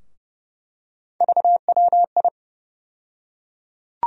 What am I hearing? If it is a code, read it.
Morse code, heard as a keyed tone, about 30 wpm, 700 Hz. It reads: VWI
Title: USA Call Sign Postfixes - 3 Letters 30wpm